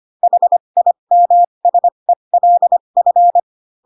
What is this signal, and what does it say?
Morse code, heard as a keyed tone, about 25 wpm, spaced as for 22 wpm, 700 Hz. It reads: HIMSELF
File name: Pinocchio / Pinocchio ch2 18 25 ik1hge